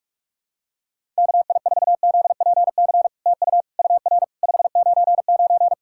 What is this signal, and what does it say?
Morse code, heard as a keyed tone, about 46 wpm, 700 Hz. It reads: KI4ZPC TU UR 599
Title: SOTA QSO Practice 45wpm